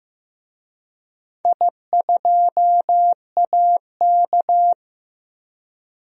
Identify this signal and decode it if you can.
Morse code, heard as a keyed tone, about 15 wpm, 700 Hz. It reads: I2AK
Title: DXCC-100 Call Signs - 4 Characters - Repeated 3x - 15wpm